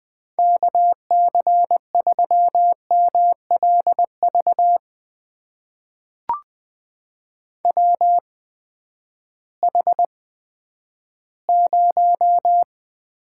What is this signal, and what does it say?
Morse code, heard as a keyed tone, about 20 wpm, spaced as for 11 wpm, 700 Hz. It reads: KC3MLV WH0
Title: USA Call Signs - 8x Character Spacing - 20wpm